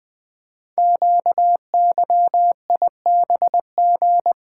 Morse code, heard as a keyed tone, about 20 wpm, 700 Hz. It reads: QYIBG
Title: Code Groups - 5 Letters - 20wpm